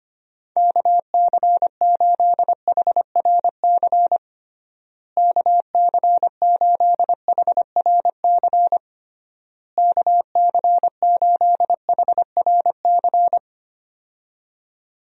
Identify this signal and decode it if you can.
Morse code, heard as a keyed tone, about 25 wpm, 700 Hz. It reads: KC8HRC KC8HRC KC8HRC
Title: USA Call Signs - 6 Characters - Repeated 3x - 25wpm